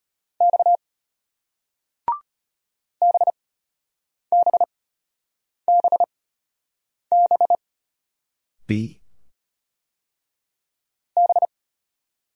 Morse code, heard as a keyed tone, about 33 wpm, spaced as for 8 wpm, 700 Hz. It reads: X BBBB B